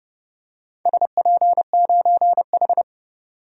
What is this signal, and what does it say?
Morse code, heard as a keyed tone, about 31 wpm, 700 Hz. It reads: SP9H
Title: DXCC-100 Call Signs - 4 Characters - 30wpm